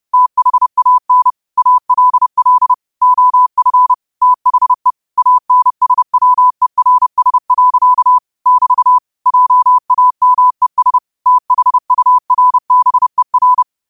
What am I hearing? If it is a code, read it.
Morse code, heard as a keyed tone, about 30 wpm, 1000 Hz. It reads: THAN ALL OF THE ANSWERS. <BT> JAMES THURBER
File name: qotdfiles / 30wpm